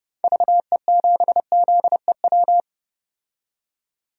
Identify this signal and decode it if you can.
Morse code, heard as a keyed tone, about 32 wpm, 700 Hz. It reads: VE7ZEW
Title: DXCC-100 Call Signs - 6 Characters - 30wpm